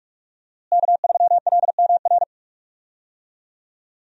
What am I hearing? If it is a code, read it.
Morse code, heard as a keyed tone, about 41 wpm, 700 Hz. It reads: K3LMR
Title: USA Call Signs - 5 Characters - Repeated 2x - 45wpm